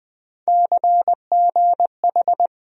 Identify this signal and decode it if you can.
Morse code, heard as a keyed tone, about 21 wpm, 700 Hz. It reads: CGH
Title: USA Call Sign Postfixes - 3 Letters 20wpm